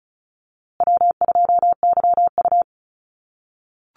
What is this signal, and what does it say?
Morse code, heard as a keyed tone, about 34 wpm, 700 Hz. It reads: W2YU
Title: USA Call Signs Repeated 3x 35wpm